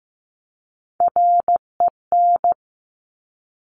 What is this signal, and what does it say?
Morse code, heard as a keyed tone, about 15 wpm, 700 Hz. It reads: REN
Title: Top Three Letter Combinations in Words 15wpm